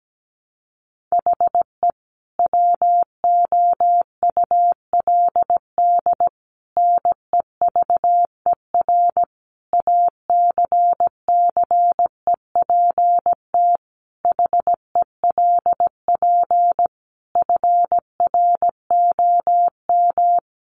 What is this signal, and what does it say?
Morse code, heard as a keyed tone, about 17 wpm, 700 Hz. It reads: HE WOULD NEVER ACCEPT HELP FROM
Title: Sentences from Top 1300 Words 17wpm